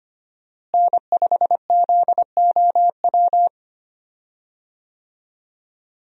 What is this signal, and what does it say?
Morse code, heard as a keyed tone, about 25 wpm, 700 Hz. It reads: N5ZOW
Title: USA Call Signs - 5 Characters - Repeated 2x - 25wpm